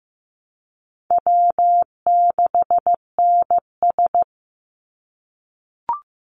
Morse code, heard as a keyed tone, about 15 wpm, 700 Hz. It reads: W6NS
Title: Easy Call Signs - 15wpm